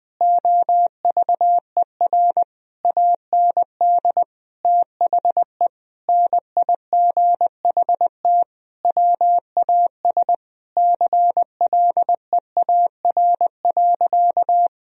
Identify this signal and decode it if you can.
Morse code, heard as a keyed tone, about 20 wpm, 700 Hz. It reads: OVER AND THE NIGHT WAS CLEAR.
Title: Sentences from Top 1000 Words - Review Part 1 - 20wpm